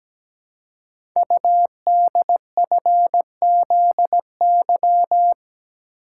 Morse code, heard as a keyed tone, about 17 wpm, 700 Hz. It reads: UDFZY
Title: Code Groups - 5 Letters - 17wpm